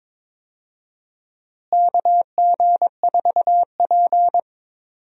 Morse code, heard as a keyed tone, about 22 wpm, 700 Hz. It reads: KG4P
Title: USA Call Signs - 4 Characters - Repeated 3x - 22wpm